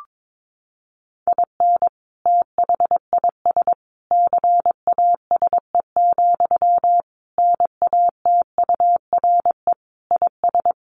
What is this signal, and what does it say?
Morse code, heard as a keyed tone, about 22 wpm, 700 Hz. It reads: IN THIS CASE, NATURE IS
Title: Sentences from Top 1400 Words 22wpm